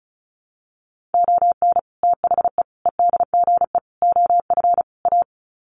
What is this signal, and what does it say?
Morse code, heard as a keyed tone, about 34 wpm, 700 Hz. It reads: ON THE EDGE OF A
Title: Sets of 5 Words Encore2 35wpm